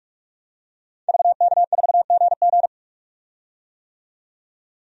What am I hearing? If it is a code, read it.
Morse code, heard as a keyed tone, about 46 wpm, 700 Hz. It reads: VK4GG
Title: DXCC-100 Call Signs - 5 Characters - 45wpm